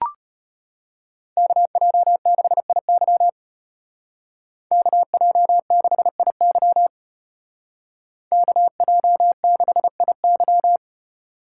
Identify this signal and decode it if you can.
Morse code, heard as a keyed tone, about 30 wpm, 700 Hz. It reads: KJ6IY KJ6IY KJ6IY
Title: Speed Racing - USA 2x2 Call Signs 25wpm